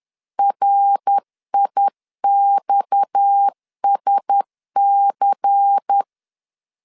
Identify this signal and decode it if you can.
Morse code, heard as a keyed tone, about 10 wpm, 800 Hz. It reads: RIXSC
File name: CW / DET / 1 9 1500C S B 10wpm 4439